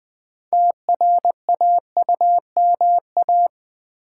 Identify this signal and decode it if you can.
Morse code, heard as a keyed tone, about 20 wpm, 700 Hz. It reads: TRAUMA
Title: Build A Word - 2 Letters At A Time - 6-Letter Words - 20wpm